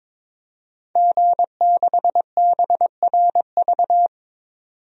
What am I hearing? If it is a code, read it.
Morse code, heard as a keyed tone, about 22 wpm, 700 Hz. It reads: G6BRV
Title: DXCC-100 Call Signs - 5 Characters - 22wpm